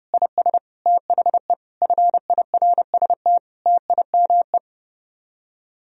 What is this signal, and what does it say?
Morse code, heard as a keyed tone, about 30 wpm, 700 Hz. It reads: IS THE FIRST TIME